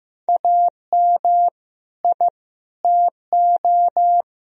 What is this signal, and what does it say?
Morse code, heard as a keyed tone, about 15 wpm, 700 Hz. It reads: AM I TO
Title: Sentences from Top 1000 Words - Review Part 2 - 15wpm